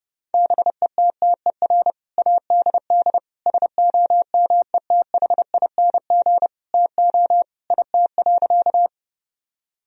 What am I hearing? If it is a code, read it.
Morse code, heard as a keyed tone, about 32 wpm, 700 Hz. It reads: BETTER ADD SOMETHING TO IT.
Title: Sentences from Top 1000 Words - Review Part 1 - 30wpm